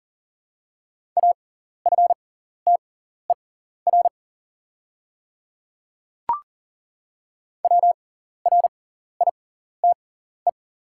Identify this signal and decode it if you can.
Morse code, heard as a keyed tone, about 39 wpm, spaced as for 14 wpm, 700 Hz. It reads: AFTER WRITE
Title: 6x Character Spacing - Top 200 Words - 40wpm